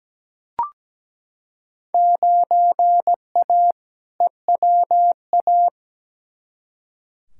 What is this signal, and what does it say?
Morse code, heard as a keyed tone, about 17 wpm, 700 Hz. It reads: 9A EWA
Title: Field Day Exchanges Encore - 17wpm